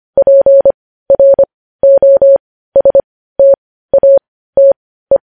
Morse code, heard as a keyed tone, about 25 wpm, spaced as for 15 wpm, 550 Hz. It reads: PROSTATE
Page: news-headlines-in-morse-code-at-15-wpm